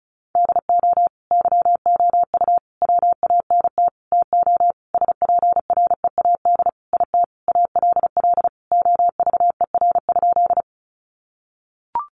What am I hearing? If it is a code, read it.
Morse code, heard as a keyed tone, about 34 wpm, 700 Hz. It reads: DO YOU WANT TO SPREAD IT ALL OVER?
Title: Sentences from Top 1200 Words 35wpm